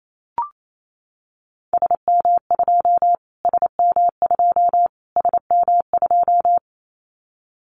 Morse code, heard as a keyed tone, about 28 wpm, 700 Hz. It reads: SM2 SM2 SM2
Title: Callsign Prefixes for Top 25 DXCC - Repeated 3x - 28wpm